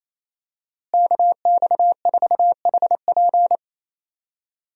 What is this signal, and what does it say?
Morse code, heard as a keyed tone, about 27 wpm, 700 Hz. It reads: KX4HP
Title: USA Call Signs - 5 Characters - Repeated 3x - 28wpm